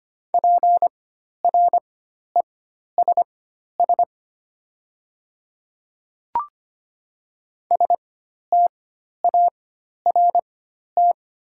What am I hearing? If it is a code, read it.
Morse code, heard as a keyed tone, about 25 wpm, spaced as for 12 wpm, 690 Hz. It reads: PRESS START
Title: 4x Character Spacing - Top 300 Words - 25wpm